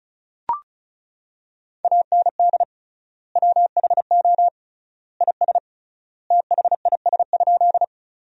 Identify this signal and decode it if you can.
Morse code, heard as a keyed tone, about 36 wpm, 700 Hz. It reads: AND WHO IS THIS?
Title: 3x Word-Spacing - Sentences from Top 100 Words - 35wpm